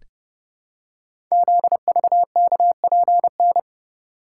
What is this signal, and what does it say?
Morse code, heard as a keyed tone, about 30 wpm, 700 Hz. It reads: ZVKPN
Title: Code Groups - 5 Letters - 30wpm